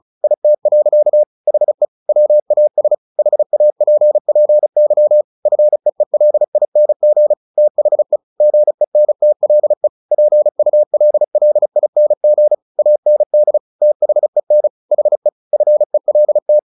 Morse code, heard as a keyed tone, about 34 wpm, 600 Hz. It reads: IT. HE WAS HAPPY FEELING THE GENTLE PULLING AND THEN HE FELT
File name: the old man and the sea / the old man and the sea 1 35wpm